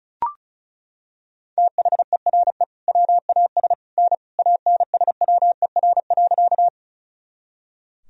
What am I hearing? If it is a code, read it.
Morse code, heard as a keyed tone, about 36 wpm, 700 Hz. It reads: THERE WAS N ANSWER.
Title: Sentences from Top 300 Words and Abbreviations 35wpm